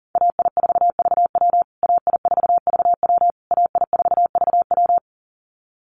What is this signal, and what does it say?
Morse code, heard as a keyed tone, about 40 wpm, 700 Hz. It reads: AI4VW AI4VW AI4VW
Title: USA Call Signs Repeated 3x 40wpm